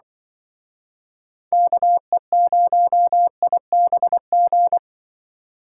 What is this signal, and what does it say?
Morse code, heard as a keyed tone, about 25 wpm, 700 Hz. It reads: KE0IBG